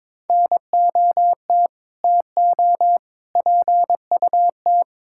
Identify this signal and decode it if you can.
Morse code, heard as a keyed tone, about 22 wpm, 700 Hz. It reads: NOT TO PUT
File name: practice / 22wpm / Sentences with 8-Word Phrases - 22wpm